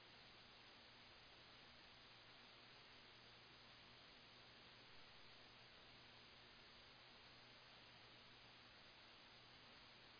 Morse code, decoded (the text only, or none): none